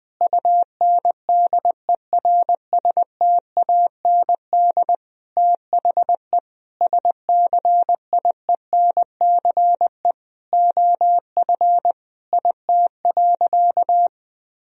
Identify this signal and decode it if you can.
Morse code, heard as a keyed tone, about 21 wpm, 700 Hz. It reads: UNDERSTAND THE SCIENCE OF IT.
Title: Sentences from Top 1000 Words - Review Part 2 - 20wpm